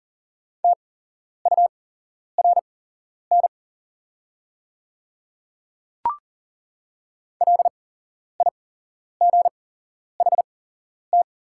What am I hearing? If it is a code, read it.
Morse code, heard as a keyed tone, about 40 wpm, spaced as for 11 wpm, 690 Hz. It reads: TURN LIGHT
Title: 8x Character Spacing - Top 200 Words - 40wpm